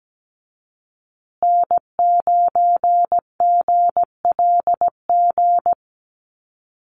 Morse code, heard as a keyed tone, about 17 wpm, 700 Hz. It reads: N9GLG
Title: USA Call Signs Repeated 2x 17wpm